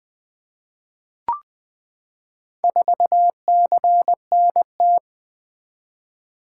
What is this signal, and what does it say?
Morse code, heard as a keyed tone, about 20 wpm, 700 Hz. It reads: 4CNT
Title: Four Letter-Number Combinations - 20wpm